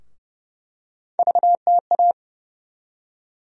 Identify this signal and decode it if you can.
Morse code, heard as a keyed tone, about 29 wpm, 700 Hz. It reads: VTA
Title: USA Call Sign Postfixes - 3 Letters 30wpm